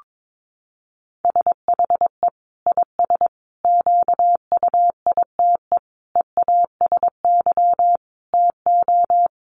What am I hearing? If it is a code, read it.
Morse code, heard as a keyed tone, about 22 wpm, 700 Hz. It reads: SHE IS QUITE EASY TO
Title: Sentences from Top 1500 Words 22wpm